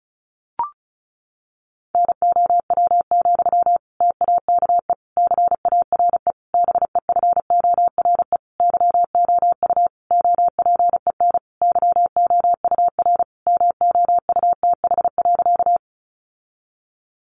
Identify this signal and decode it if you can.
Morse code, heard as a keyed tone, about 34 wpm, 700 Hz. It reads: NOW, TAKE CARE BEFORE YOU OPEN YOUR MOUTH.
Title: Sentences from Top 1500 Words 35wpm